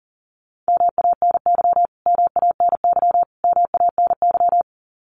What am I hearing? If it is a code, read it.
Morse code, heard as a keyed tone, about 40 wpm, 700 Hz. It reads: MANY MANY MANY
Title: Top 500 Words Repeated 40wpm